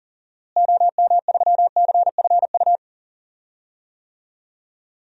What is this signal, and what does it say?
Morse code, heard as a keyed tone, about 40 wpm, 700 Hz. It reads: OM3CFU
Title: DXCC-100 Call Signs - 6 Characters - Repeated 3x - 40wpm